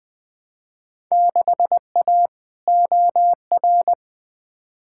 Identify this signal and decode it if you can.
Morse code, heard as a keyed tone, about 21 wpm, 700 Hz. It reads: 6A OR
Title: Field Day Exchanges - 20wpm